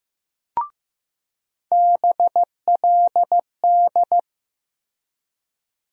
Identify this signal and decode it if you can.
Morse code, heard as a keyed tone, about 15 wpm, 700 Hz. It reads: BLD
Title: USA Call Sign Postfixes - 3 Letters 15wpm